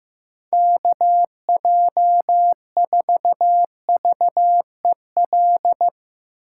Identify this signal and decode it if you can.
Morse code, heard as a keyed tone, about 16 wpm, 700 Hz. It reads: KJ4VEL
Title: Speed Racing - USA 2x3 Call Signs 15wpm